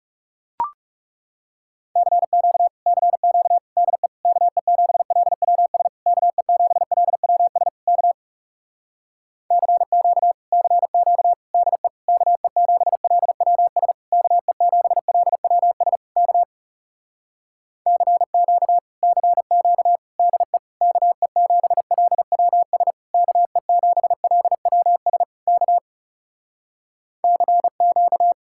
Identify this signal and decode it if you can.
Morse code, heard as a keyed tone, about 40 wpm, 700 Hz. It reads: CQ CQ DE KE7LWS KE7LWS K CQ CQ DE KE7LWS KE7LWS K CQ CQ DE KE7LWS KE7LWS K CQ